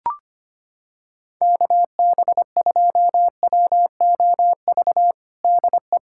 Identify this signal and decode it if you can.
Morse code, heard as a keyed tone, about 25 wpm, 700 Hz. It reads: KB2WOV DE
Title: USA Callsign DE Callsign - 6 Characters - 25wpm